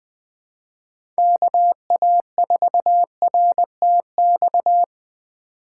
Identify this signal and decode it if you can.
Morse code, heard as a keyed tone, about 20 wpm, 700 Hz. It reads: KA4RTX
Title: Speed Racing - USA 2x3 Call Signs 17wpm